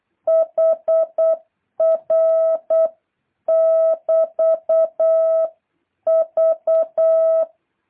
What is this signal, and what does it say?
Morse code, heard as a keyed tone, about 8 wpm, 650 Hz. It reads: HR <BT>V
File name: jh8cbh / 2023cw